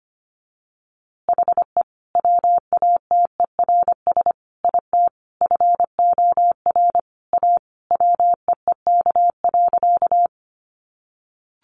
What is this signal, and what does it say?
Morse code, heard as a keyed tone, about 25 wpm, 700 Hz. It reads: HE WATERS IT FOR A WEEK.
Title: Easy Phrases 25wpm